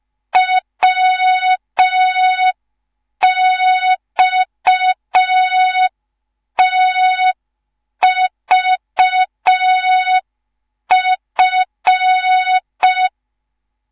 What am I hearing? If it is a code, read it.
Morse code, heard as a keyed tone, about 5 wpm, 3700 Hz. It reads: WXTVF